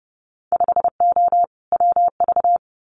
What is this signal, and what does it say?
Morse code, heard as a keyed tone, about 32 wpm, 700 Hz. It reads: 5O WV
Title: Winter Field Day Exchanges Encore 30wpm